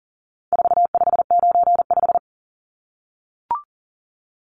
Morse code, heard as a keyed tone, about 40 wpm, 700 Hz. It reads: 4595
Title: Numbers - 4-Digit 40wpm